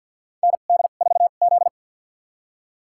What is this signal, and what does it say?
Morse code, heard as a keyed tone, about 52 wpm, spaced as for 36 wpm, 700 Hz. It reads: ND4Z
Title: USA Call Signs - 4 Characters - 2x Character Spacing - 50wpm